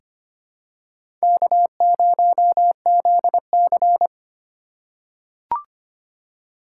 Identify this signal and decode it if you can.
Morse code, heard as a keyed tone, about 25 wpm, 700 Hz. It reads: K0ZC
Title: USA Call Signs - 3x Character Spacing - 25wpm